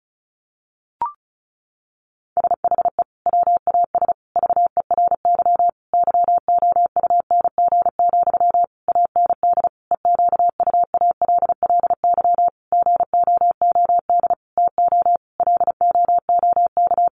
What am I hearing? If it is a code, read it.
Morse code, heard as a keyed tone, about 34 wpm, 700 Hz. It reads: SHE WAS VERY YOUNG, AND EQUALLY GOOD TO LOOK
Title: Sentences from Top 1400 Words 35wpm